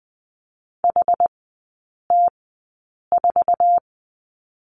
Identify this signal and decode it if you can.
Morse code, heard as a keyed tone, about 20 wpm, spaced as for 8 wpm, 700 Hz. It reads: HT4
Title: 100 Lesson 17e - Letter u - Three Characters 20wpm